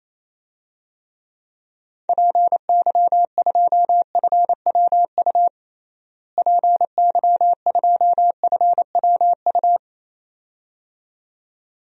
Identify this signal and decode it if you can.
Morse code, heard as a keyed tone, about 28 wpm, 700 Hz. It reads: PY2FWU PY2FWU
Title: DXCC-100 Call Signs - 6 Characters - Repeated 2x - 28wpm